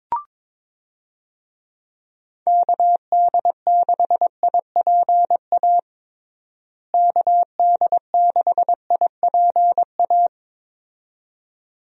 Morse code, heard as a keyed tone, about 22 wpm, 700 Hz. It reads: KD6IPA KD6IPA
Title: USA Call Signs - 6 Characters - Repeated 2x - 22wpm